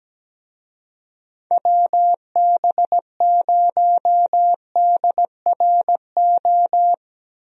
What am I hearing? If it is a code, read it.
Morse code, heard as a keyed tone, about 17 wpm, 700 Hz. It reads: WB0DRO
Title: USA Call Signs - 6 Characters - 17wpm